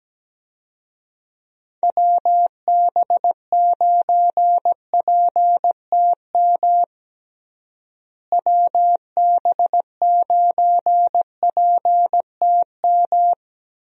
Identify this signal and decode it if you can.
Morse code, heard as a keyed tone, about 17 wpm, 700 Hz. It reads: WB9PTM WB9PTM
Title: USA Call Signs - 6 Characters - Repeated 3x - 17wpm